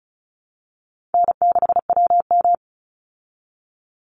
Morse code, heard as a keyed tone, about 34 wpm, 700 Hz. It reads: N6WM
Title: Easy Call Signs Repeated 3x - 35wpm